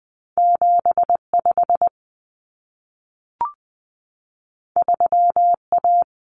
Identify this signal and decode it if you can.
Morse code, heard as a keyed tone, about 20 wpm, 700 Hz. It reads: 75 3A
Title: Common Cut Numbers - 2-Digit 20wpm